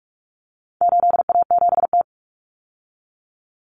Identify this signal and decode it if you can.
Morse code, heard as a keyed tone, about 44 wpm, 700 Hz. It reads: 8A7T
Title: Common Cut Numbers - 4-Digit 45wpm